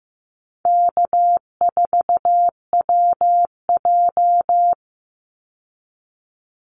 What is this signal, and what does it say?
Morse code, heard as a keyed tone, about 15 wpm, 700 Hz. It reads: K4WJ
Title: Easy Call Signs Repeated 2x - 15wpm